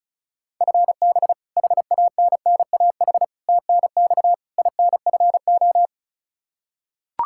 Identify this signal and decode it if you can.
Morse code, heard as a keyed tone, about 34 wpm, 700 Hz. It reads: FB HANNAH TNX INFO
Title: Practice QSO - Exchange 3 - More Info - 35wpm